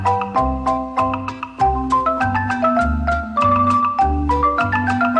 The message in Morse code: none